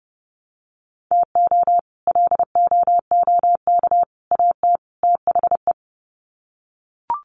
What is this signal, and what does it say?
Morse code, heard as a keyed tone, about 30 wpm, 700 Hz. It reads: TO LOOK AT THE
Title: Sets of 4 Words 30wpm